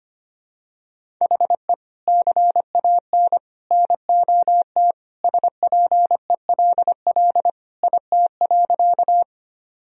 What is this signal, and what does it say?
Morse code, heard as a keyed tone, about 25 wpm, 700 Hz. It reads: HE CAN NOT SPELL IT.
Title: Sentences from Top 1000 Words - Review Part 1 - 25wpm